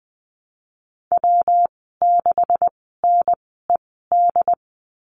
Morse code, heard as a keyed tone, about 20 wpm, spaced as for 14 wpm, 700 Hz. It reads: W6NED
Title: 186 Lesson 30g - Letter b - Call Signs 20wpm